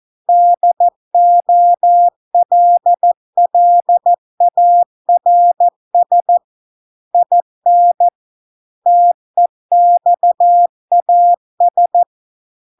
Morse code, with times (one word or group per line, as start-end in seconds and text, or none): 0.3-6.4 DOLLARS
7.1-8.1 IN
8.9-12.0 TEXAS